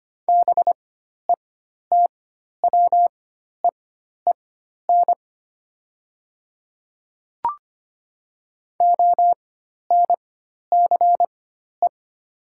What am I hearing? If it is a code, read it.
Morse code, heard as a keyed tone, about 25 wpm, spaced as for 12 wpm, 690 Hz. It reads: BETWEEN ONCE
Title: 4x Character Spacing - Top 300 Words - 25wpm